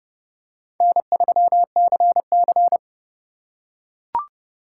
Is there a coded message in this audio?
Yes: Morse code from 0.8 to 2.8 s, heard as a keyed tone, about 27 wpm, 700 Hz. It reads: N3CC